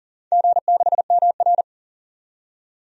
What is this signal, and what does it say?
Morse code, heard as a keyed tone, about 40 wpm, 700 Hz. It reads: G6MR
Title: DXCC-100 Call Signs - 4 Characters - 40wpm